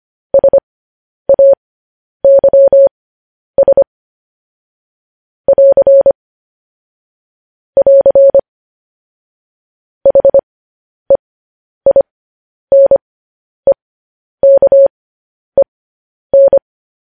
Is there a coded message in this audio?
Yes: Morse code from 0.3 to 16.6 s, heard as a keyed tone, about 25 wpm, spaced as for 10 wpm, 550 Hz. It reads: SAYS <AR> <AR> HEINEKEN